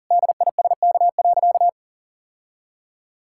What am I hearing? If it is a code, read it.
Morse code, heard as a keyed tone, about 40 wpm, 700 Hz. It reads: DISK.
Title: Sentences from Top 1000 Words - Review Part 6 - 40wpm